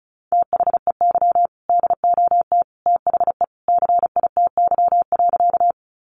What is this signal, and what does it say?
Morse code, heard as a keyed tone, about 34 wpm, 700 Hz. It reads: THEY DOT THE CITY.
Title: Sentences from Top 1500 Words 35wpm